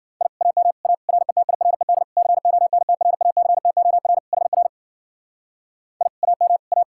Morse code, heard as a keyed tone, about 51 wpm, 700 Hz. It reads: I AM A RETIRED CONTRACTOR HR I AM A